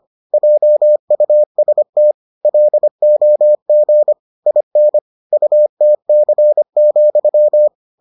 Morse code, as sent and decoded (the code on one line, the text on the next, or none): .--- ..- ... - / .-.. --- --. / .. -. / ..- - -.-. --..--
JUST LOG IN UTC,